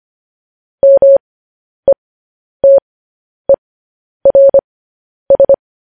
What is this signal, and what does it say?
Morse code, heard as a keyed tone, about 25 wpm, spaced as for 10 wpm, 550 Hz. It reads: METERS
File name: podcast / News headlines-1x-25x10